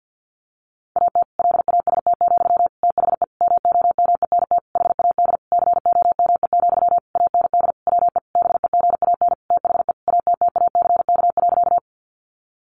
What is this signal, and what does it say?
Morse code, heard as a keyed tone, about 50 wpm, 700 Hz. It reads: AT LAST, THE MOMENT HAD COME, AND WE BEGAN THE ATTACK.
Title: Sentences from Top 1500 Words 50wpm